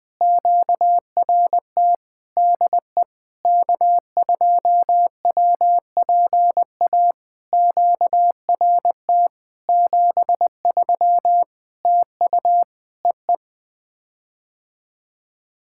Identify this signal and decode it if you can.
Morse code, heard as a keyed tone, about 20 wpm, 700 Hz. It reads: QRT DE K2WPA QRT 73 TU EE
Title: POTA QSO Practice - 20wpm